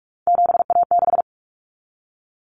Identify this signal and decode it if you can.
Morse code, heard as a keyed tone, about 44 wpm, 700 Hz. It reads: 7A6
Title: Common Cut Numbers - 3-Digit 45wpm